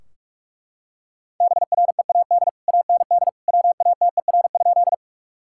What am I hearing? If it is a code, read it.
Morse code, heard as a keyed tone, about 43 wpm, 700 Hz. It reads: BREAD AND WATER?
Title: Sentences from Top 1600 Words 45wpm